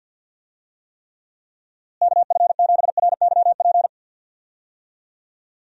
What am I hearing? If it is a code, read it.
Morse code, heard as a keyed tone, about 53 wpm, 700 Hz. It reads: KF7RYP
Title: USA Call Signs - 6 Characters - Repeated 3x - 50wpm